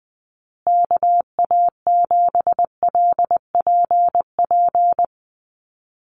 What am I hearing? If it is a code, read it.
Morse code, heard as a keyed tone, about 19 wpm, 700 Hz. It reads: KA7LPP
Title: USA Call Signs Repeated 2x 20wpm